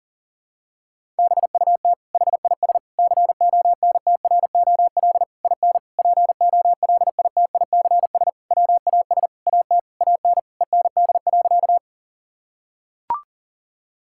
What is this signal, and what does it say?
Morse code, heard as a keyed tone, about 38 wpm, 700 Hz. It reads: BUT HIS CONTROL IN POLITICS WAS AT AN END.
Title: Sentences from Top 1000 Words - Review Part 6 - 40wpm